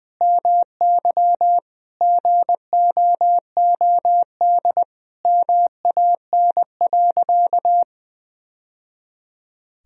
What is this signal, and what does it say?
Morse code, heard as a keyed tone, about 21 wpm, 700 Hz. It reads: MY GOOD MAN.
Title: Sentences from Top 1000 Words - Review Part 1 - 20wpm